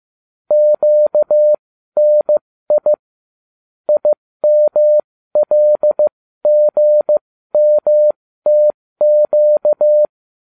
Morse code, heard as a keyed tone, about 15 wpm, spaced as for 13 wpm, 600 Hz. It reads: QNI IMLGMTQ